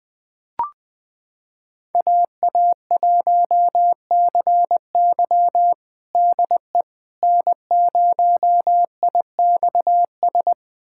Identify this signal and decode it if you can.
Morse code, heard as a keyed tone, about 20 wpm, 700 Hz. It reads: AA1CY DE N0IXS
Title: USA Callsign DE Callsign - 5 Characters - 20wpm